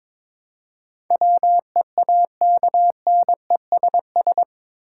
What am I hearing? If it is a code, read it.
Morse code, heard as a keyed tone, about 21 wpm, 700 Hz. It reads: WEAKNESS
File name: practice / 22wpm / Eight Letter Words - 22wpm